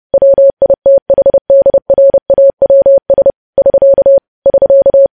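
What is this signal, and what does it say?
Morse code, heard as a keyed tone, about 30 wpm, 550 Hz. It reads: WITHDRAWS <SK> <SK>
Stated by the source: Morse news headlines sent at 30.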